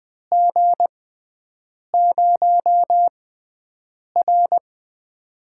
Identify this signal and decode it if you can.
Morse code, heard as a keyed tone, about 20 wpm, spaced as for 7 wpm, 700 Hz. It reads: G0R
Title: USA Call Signs - 4 Characters - 6x Character Spacing - 20wpm